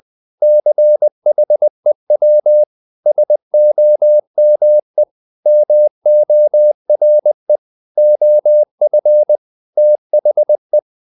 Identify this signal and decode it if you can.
Morse code, heard as a keyed tone, about 20 wpm, 600 Hz. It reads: CHEW SOME MORE OF THE
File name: the old man and the sea part 2 20wpm